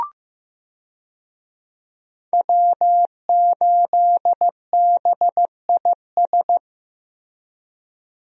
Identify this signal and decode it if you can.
Morse code, heard as a keyed tone, about 15 wpm, 700 Hz. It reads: W8BIS
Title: USA Call Signs - 5 Characters - Repeated 3x - 15wpm